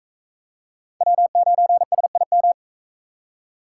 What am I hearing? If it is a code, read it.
Morse code, heard as a keyed tone, about 38 wpm, 700 Hz. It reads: W9SIM